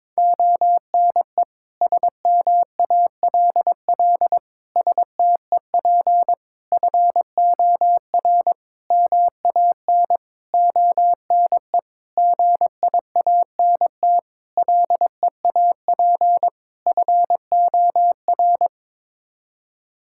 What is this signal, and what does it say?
Morse code, heard as a keyed tone, about 22 wpm, 700 Hz. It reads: ONE SMALL STEP FOR MAN ONE GIANT LEAP FOR
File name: Sets of 9 Words - 22wpm